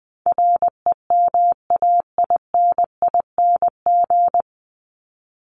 Morse code, heard as a keyed tone, about 20 wpm, 700 Hz. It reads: REMAINING
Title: ING Suffix - Top 100 Words with Base Words 20wpm